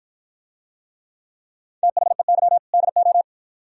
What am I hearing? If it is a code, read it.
Morse code, heard as a keyed tone, about 53 wpm, 700 Hz. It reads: THEY DO